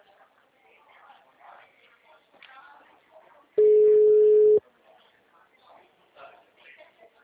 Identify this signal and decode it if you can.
Ringback tone (Europe, 425 Hz), with speech.
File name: out- -2019